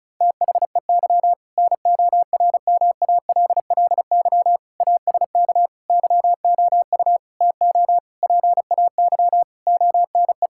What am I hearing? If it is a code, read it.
Morse code, heard as a keyed tone, about 36 wpm, 700 Hz. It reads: THEY NORMALLY ASK YOU TO PAY ONE
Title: Sentences from Top 1000 Words - Review Part 6 - 35wpm